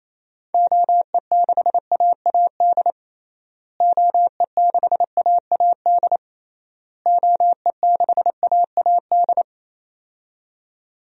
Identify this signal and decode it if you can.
Morse code, heard as a keyed tone, about 28 wpm, 700 Hz. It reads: OE6AAD OE6AAD OE6AAD
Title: DXCC-100 Call Signs - 6 Characters - Repeated 3x - 28wpm